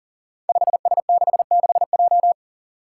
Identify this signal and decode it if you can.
Morse code, heard as a keyed tone, about 42 wpm, 700 Hz. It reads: 5S66J
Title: Five Challenging Letter-Number Combinations 40wpm